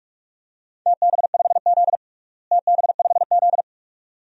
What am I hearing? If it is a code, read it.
Morse code, heard as a keyed tone, about 41 wpm, 700 Hz. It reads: TB5Z TB5Z
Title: DXCC-100 Call Signs - 4 Characters - Repeated 2x - 45wpm